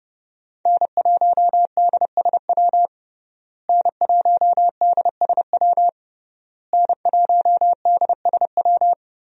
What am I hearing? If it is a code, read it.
Morse code, heard as a keyed tone, about 31 wpm, 700 Hz. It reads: N1DSW N1DSW N1DSW
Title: USA Call Signs - 5 Characters - Repeated 3x - 30wpm